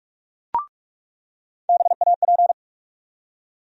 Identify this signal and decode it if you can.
Morse code, heard as a keyed tone, about 42 wpm, 700 Hz. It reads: BAP
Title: USA Call Sign Postfixes - 3 Letters 45wpm